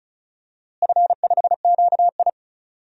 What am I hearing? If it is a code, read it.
Morse code, heard as a keyed tone, about 34 wpm, 700 Hz. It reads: F5QI
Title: DXCC-100 Call Signs - 4 Characters - Repeated 3x - 35wpm